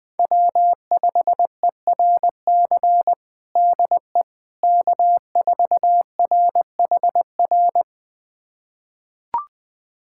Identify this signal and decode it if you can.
Morse code, heard as a keyed tone, about 20 wpm, 700 Hz. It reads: W5ERC DE K4RHR
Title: USA Callsign DE Callsign 20wpm